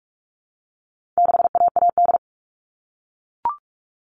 Morse code, heard as a keyed tone, about 44 wpm, 700 Hz. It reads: 6AAD
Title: All Cut Numbers - 4-Digit 45wpm